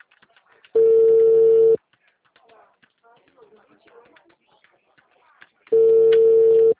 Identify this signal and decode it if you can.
Ringback tone (China, 450 Hz), with speech.
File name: out- -2009- - - .25697